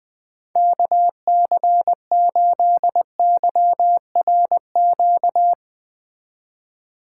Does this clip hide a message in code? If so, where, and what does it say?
Yes: Morse code from 0.6 to 5.5 s, heard as a keyed tone, about 21 wpm, 700 Hz. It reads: KC8YRQ